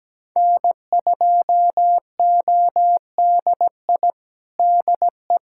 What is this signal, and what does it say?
Morse code, heard as a keyed tone, about 17 wpm, 700 Hz. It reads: N2ODI DE
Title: Practice QSO - Exchange 4 - Ending - 17wpm